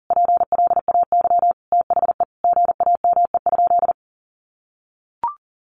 Morse code, heard as a keyed tone, about 40 wpm, 700 Hz. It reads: PLAY THE GAME?
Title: Sentences from Top 1400 Words 40wpm